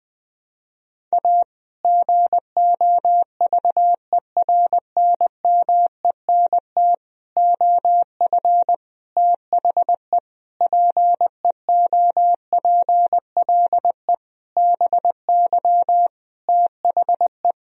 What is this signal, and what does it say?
Morse code, heard as a keyed tone, about 20 wpm, 700 Hz. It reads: A GOVERNMENT OF THE PEOPLE BY THE
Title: Sentences from Top 1000 Words - Review Part 3 - 20wpm